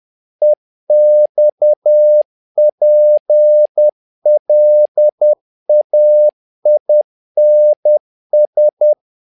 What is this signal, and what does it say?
Morse code, heard as a keyed tone, about 10 wpm, 600 Hz. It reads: EXPLAINS
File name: learning center / tower trouble / tower trouble 10wpm